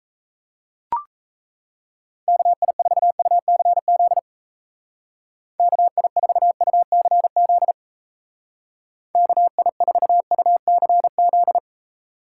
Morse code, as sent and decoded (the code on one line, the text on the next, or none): -.- .. ....- ..- -.-. --.. / -.- .. ....- ..- -.-. --.. / -.- .. ....- ..- -.-. --..
KI4UCZ KI4UCZ KI4UCZ